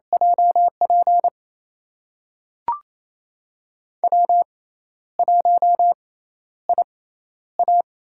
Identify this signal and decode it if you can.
Morse code, heard as a keyed tone, about 28 wpm, spaced as for 12 wpm, 700 Hz. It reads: JP W1IA
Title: USA Call Signs - 6x Character Spacing - 28wpm